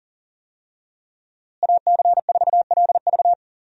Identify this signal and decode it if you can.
Morse code, heard as a keyed tone, about 37 wpm, 700 Hz. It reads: AC4LV
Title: USA Call Signs - 5 Characters - Repeated 2x - 40wpm